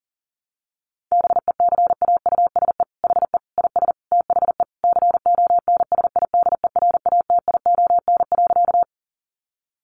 Morse code, heard as a keyed tone, about 40 wpm, 700 Hz. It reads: BECAUSE HE IS THE CONSIDERATION.
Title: Sentences from Top 1500 Words 40wpm